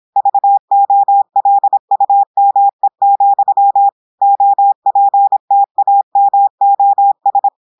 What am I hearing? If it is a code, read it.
Morse code, heard as a keyed tone, about 26 wpm, 800 Hz. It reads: VOLUME, OPTAMOS
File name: audios / portugais / portugais texte en clair 1 vitesse 26 mots minutes